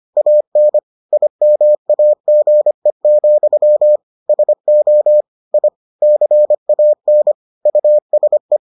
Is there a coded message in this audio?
Yes: Morse code from 0.2 to 8.6 s, heard as a keyed tone, about 25 wpm, 600 Hz. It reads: AN IMAGE, SO I CAN USE